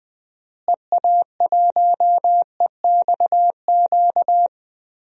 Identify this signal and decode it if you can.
Morse code, heard as a keyed tone, about 20 wpm, 700 Hz. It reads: EA1EXQ
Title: DXCC-100 Call Signs - 6 Characters - Repeated 2x - 20wpm